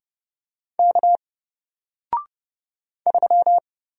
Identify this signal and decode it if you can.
Morse code, heard as a keyed tone, about 30 wpm, 700 Hz. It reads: K 3
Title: Single Letter-Number Alt 30wpm